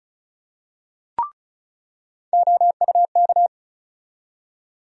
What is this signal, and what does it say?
Morse code, heard as a keyed tone, about 36 wpm, 700 Hz. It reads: OUK
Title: USA Call Sign Postfixes - 3 Letters 35wpm